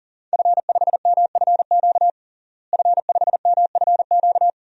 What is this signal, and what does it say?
Morse code, heard as a keyed tone, about 40 wpm, 700 Hz. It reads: F5MFQ F5MFQ
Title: DXCC-100 Call Signs - 5 Characters - Repeated 2x - 40wpm